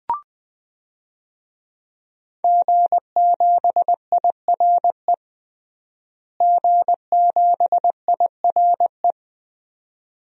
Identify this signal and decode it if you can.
Morse code, heard as a keyed tone, about 21 wpm, 700 Hz. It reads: G7IRE G7IRE
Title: DXCC-100 Call Signs - 5 Characters - Repeated 2x - 20wpm